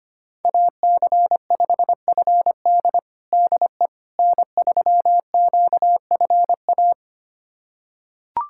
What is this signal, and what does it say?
Morse code, heard as a keyed tone, about 25 wpm, 700 Hz. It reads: AC5FD DE N3QFA
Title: USA Callsign DE Callsign - 5 Characters - 25wpm